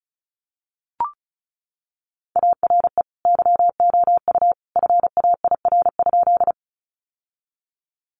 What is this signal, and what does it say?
Morse code, heard as a keyed tone, about 34 wpm, 700 Hz. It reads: ARE YOU FAIR?
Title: Sentences from Top 1300 Words 35wpm